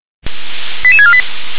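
Five-tone selective call (ZVEI), heard as quick runs of five tones, tones of about 70 ms. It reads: 88648
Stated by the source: ZVEI five-tone selective call received by SDR.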